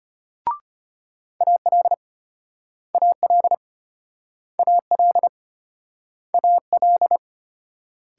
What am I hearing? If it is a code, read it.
Morse code, heard as a keyed tone, about 35 wpm, 700 Hz. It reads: AL AL AL AL